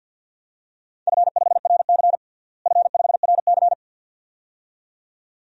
Morse code, heard as a keyed tone, about 51 wpm, 700 Hz. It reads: F5RC F5RC
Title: DXCC-100 Call Signs - 4 Characters - Repeated 2x - 50wpm